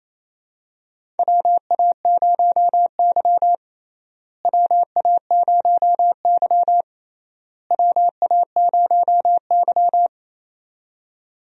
Morse code, heard as a keyed tone, about 28 wpm, 700 Hz. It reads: WA0Y WA0Y WA0Y
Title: USA Call Signs - 4 Characters - Repeated 3x - 28wpm